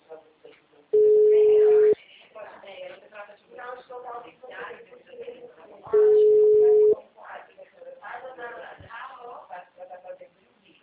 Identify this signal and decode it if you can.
Ringback tone (Europe, 425 Hz), with speech.